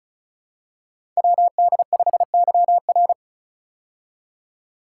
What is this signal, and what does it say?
Morse code, heard as a keyed tone, about 36 wpm, 700 Hz. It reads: WD5YR
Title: Code Groups - 5 Letter:Numbers - 35wpm